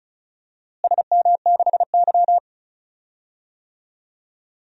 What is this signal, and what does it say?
Morse code, heard as a keyed tone, about 36 wpm, 700 Hz. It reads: SM6Y
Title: DXCC-100 Call Signs - 4 Characters - Repeated 3x - 35wpm